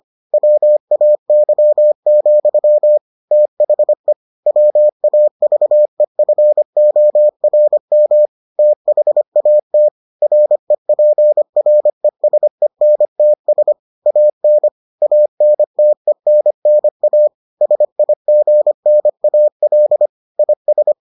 Morse code, heard as a keyed tone, about 25 wpm, 600 Hz. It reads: WAY, THE WAVEFORM THAT REPRESENTS AN ANTENNA SIGNAL IS